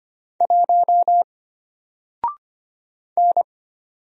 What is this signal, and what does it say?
Morse code, heard as a keyed tone, about 25 wpm, 700 Hz. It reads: 1 N
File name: Single-Letter-Number-25wpm